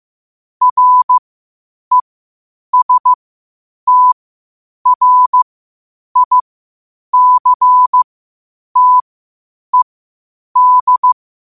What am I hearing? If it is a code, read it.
Morse code, heard as a keyed tone, about 15 wpm, spaced as for 8 wpm, 1000 Hz. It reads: RESTRICTED
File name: qotdfiles / 8wpm